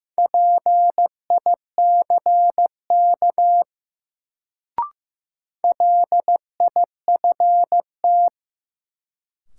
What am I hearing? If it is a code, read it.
Morse code, heard as a keyed tone, about 15 wpm, 700 Hz. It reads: PICK LIFT
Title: Sentences from Top 1000 Words - New Word Drill - 15wpm